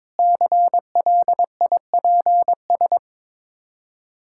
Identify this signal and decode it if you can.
Morse code, heard as a keyed tone, about 22 wpm, 690 Hz. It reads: CLIPS
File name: Difficult Words with Hints - 22wpm